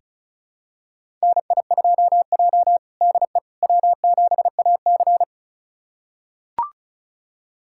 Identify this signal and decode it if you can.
Morse code, heard as a keyed tone, about 34 wpm, 700 Hz. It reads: NI2J DE W7AC
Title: USA Callsign DE Callsign - 4 Characters - 35wpm